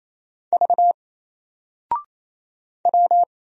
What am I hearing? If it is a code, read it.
Morse code, heard as a keyed tone, about 28 wpm, 700 Hz. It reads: V W
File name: Single-Letter-Number-28wpm